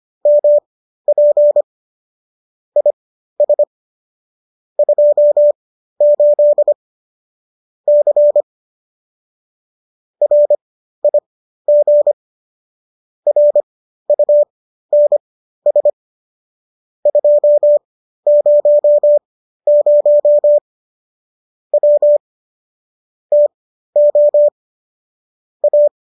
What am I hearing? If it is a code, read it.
Morse code, heard as a keyed tone, about 25 wpm, spaced as for 13 wpm, 590 Hz. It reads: MP IS 28 C RIG RUNS 200 W TO A